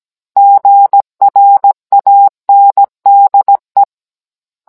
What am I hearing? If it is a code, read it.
Morse code, heard as a keyed tone, about 17 wpm, 800 Hz. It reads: GRANDE